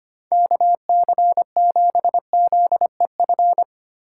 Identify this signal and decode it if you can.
Morse code, heard as a keyed tone, about 25 wpm, 700 Hz. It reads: KC7ZEF
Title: USA Call Signs - 6 Characters - 4x Character Spacing - 25wpm